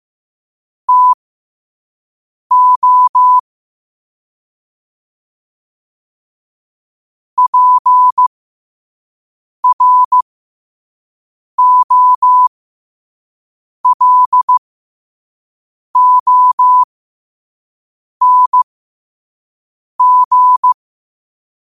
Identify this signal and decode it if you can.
Morse code, heard as a keyed tone, about 15 wpm, spaced as for 5 wpm, 1000 Hz. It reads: TO PROLONG
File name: qotdfiles / 5wpm